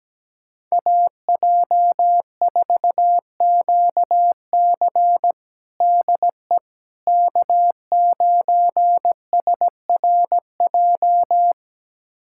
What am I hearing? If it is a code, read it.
Morse code, heard as a keyed tone, about 17 wpm, 700 Hz. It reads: AJ4QC DE K9SRJ
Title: USA Callsign DE Callsign - 5 Characters - 17wpm